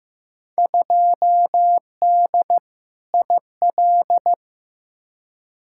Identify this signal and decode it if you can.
Morse code, heard as a keyed tone, about 15 wpm, 700 Hz. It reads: 2D IL
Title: Field Day Exchanges - 15wpm